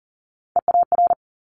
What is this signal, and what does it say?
Morse code, heard as a keyed tone, about 40 wpm, 700 Hz. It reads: EAR
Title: Multi-letter Phonemes Quick Fire 40wpm